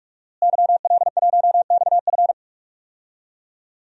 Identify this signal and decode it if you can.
Morse code, heard as a keyed tone, about 46 wpm, 700 Hz. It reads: YL1XF
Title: DXCC-100 Call Signs - 5 Characters - Repeated 3x - 45wpm